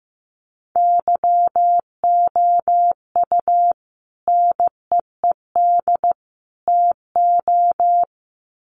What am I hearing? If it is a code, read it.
Morse code, heard as a keyed tone, about 15 wpm, 700 Hz. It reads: YOU NEED TO
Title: Sets of 3 Words 15wpm